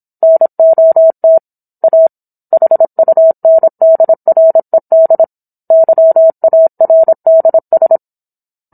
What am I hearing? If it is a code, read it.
Morse code, heard as a keyed tone, about 26 wpm, 650 Hz. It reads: NOT A HUNDRED YARDS